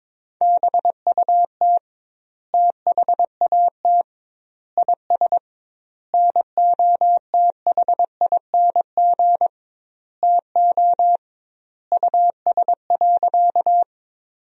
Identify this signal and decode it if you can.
Morse code, heard as a keyed tone, about 22 wpm, 700 Hz. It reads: BUT THAT IS NOTHING TO US.
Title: 2x Word-Spacing - Sentences from Top 500 Words - 22wpm